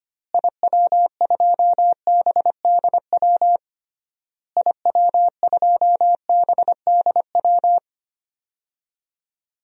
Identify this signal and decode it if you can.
Morse code, heard as a keyed tone, about 25 wpm, 700 Hz. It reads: IW2BDW IW2BDW
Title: DXCC-100 Call Signs - 6 Characters - Repeated 3x - 25wpm